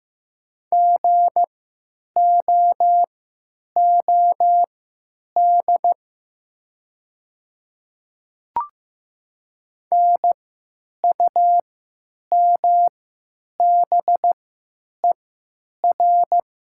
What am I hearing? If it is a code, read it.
Morse code, heard as a keyed tone, about 15 wpm, spaced as for 9 wpm, 700 Hz. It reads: GOOD NUMBER
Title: 3x Character Spacing - Top 200 Words - 15wpm